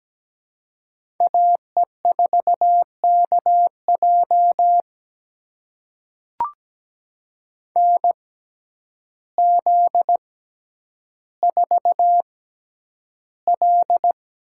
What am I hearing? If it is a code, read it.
Morse code, heard as a keyed tone, about 17 wpm, spaced as for 9 wpm, 700 Hz. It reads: AE4KJ NZ4L
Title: USA Call Signs - 6x Character Spacing - 17wpm